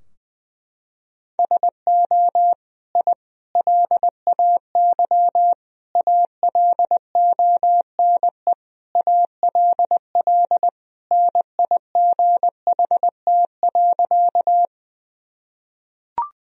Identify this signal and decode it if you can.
Morse code, heard as a keyed tone, about 21 wpm, 700 Hz. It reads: SO I LAY ALONE ALL NIGHT.
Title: Sentences from Top 1000 Words - Review Part 2 - 20wpm